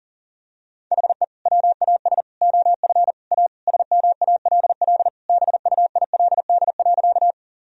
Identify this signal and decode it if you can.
Morse code, heard as a keyed tone, about 40 wpm, 700 Hz. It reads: HE WAS OF A SMALL BUILD.
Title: Sentences from Top 1000 Words - Review Part 1 - 40wpm